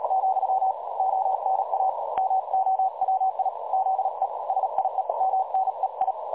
Morse code, heard as a keyed tone, about 38 wpm, 760 Hz, with noise. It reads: , ONA MOWI ZE SIE MNEI
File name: dlazaawansowanych / nagraniazpasma / websdr recording 3565.0kHz